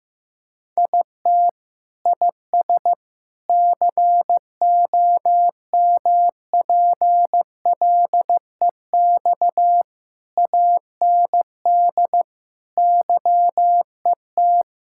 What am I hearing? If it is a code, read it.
Morse code, heard as a keyed tone, about 15 wpm, 700 Hz. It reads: IT IS COMPLEX AND YET
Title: Sentences from Top 1000 Words - Review Part 6 - 15wpm